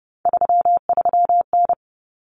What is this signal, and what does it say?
Morse code, heard as a keyed tone, about 30 wpm, 700 Hz. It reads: 33N
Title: Common Cut Numbers - 3-Digit 30wpm